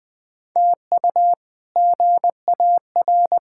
Morse code, heard as a keyed tone, about 21 wpm, 700 Hz. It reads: TU GAR
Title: K1USN SST QSO Practice - 20wpm